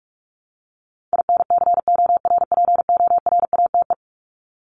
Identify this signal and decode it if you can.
Morse code, heard as a keyed tone, about 44 wpm, 700 Hz. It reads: INCORPORATE
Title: Top 1751-2000 Words 45wpm